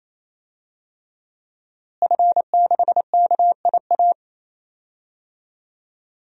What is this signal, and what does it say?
Morse code, heard as a keyed tone, about 27 wpm, 700 Hz. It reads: F6KIA
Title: DXCC-100 Call Signs - 5 Characters - Repeated 3x - 28wpm